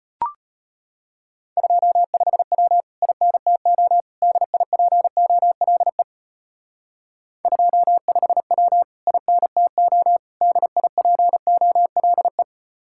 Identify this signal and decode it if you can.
Morse code, heard as a keyed tone, about 32 wpm, 700 Hz. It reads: 25W INTO DIPOLE 25W INTO DIPOLE